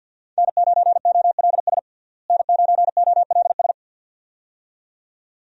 Morse code, heard as a keyed tone, about 53 wpm, 700 Hz. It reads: N9OLS N9OLS
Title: USA Call Signs - 5 Characters - Repeated 2x - 50wpm